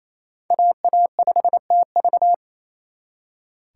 Morse code, heard as a keyed tone, about 35 wpm, 700 Hz. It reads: AA5TV
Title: Speed Racing - USA 2x2 Call Signs 28wpm